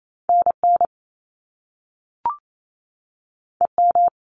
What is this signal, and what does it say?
Morse code, heard as a keyed tone, about 28 wpm, 700 Hz. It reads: N N E M
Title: Top Two Letter Combinations in Words 28wpm